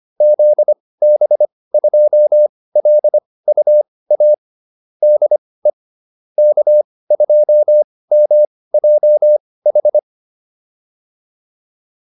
Morse code, heard as a keyed tone, about 25 wpm, spaced as for 18 wpm, 600 Hz. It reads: ZB2LUA DE K2MJH